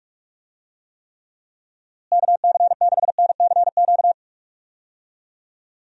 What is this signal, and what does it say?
Morse code, heard as a keyed tone, about 43 wpm, 700 Hz. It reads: KC6NCQ
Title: USA Call Signs - 6 Characters - Repeated 2x - 45wpm